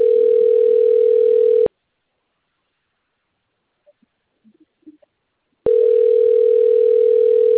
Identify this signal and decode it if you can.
Ringback tone (North America), 440 + 480 Hz.